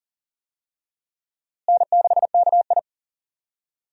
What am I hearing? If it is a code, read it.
Morse code, heard as a keyed tone, about 40 wpm, 700 Hz. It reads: N6KI
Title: USA Call Signs - 4 Characters - Repeated 3x - 40wpm